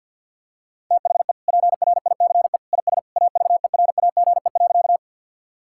Morse code, heard as a keyed tone, about 51 wpm, 700 Hz. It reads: THE PRICE IS AVERAGE.
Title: Sentences from Top 1000 Words - Review Part 6 - 50wpm